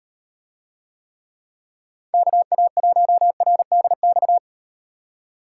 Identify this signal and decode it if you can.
Morse code, heard as a keyed tone, about 38 wpm, 700 Hz. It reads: KA1RDX